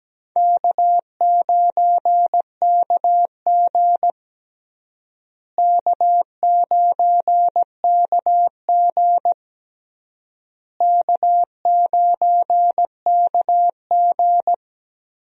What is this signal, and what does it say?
Morse code, heard as a keyed tone, about 17 wpm, 700 Hz. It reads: K9KG K9KG K9KG
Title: USA Call Signs - 4 Characters - Repeated 3x - 17wpm